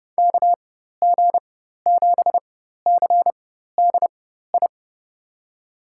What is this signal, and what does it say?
Morse code, heard as a keyed tone, about 31 wpm, spaced as for 14 wpm, 700 Hz. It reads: KG7CDI
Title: USA Call Signs - 6 Characters - 4x Character Spacing - 30wpm